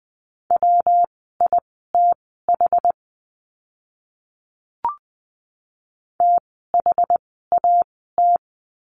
Morse code, heard as a keyed tone, about 20 wpm, spaced as for 14 wpm, 700 Hz. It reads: WITH THAT
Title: 160 Lesson 26f - Letter g - Words 20wpm